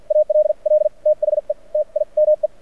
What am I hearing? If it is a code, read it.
Morse code, heard as a keyed tone, about 51 wpm, spaced as for 41 wpm, 610 Hz. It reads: ALL THE TIME
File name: archived / CRSnet